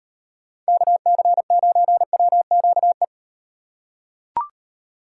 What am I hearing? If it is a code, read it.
Morse code, heard as a keyed tone, about 38 wpm, 700 Hz. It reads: KC9WQE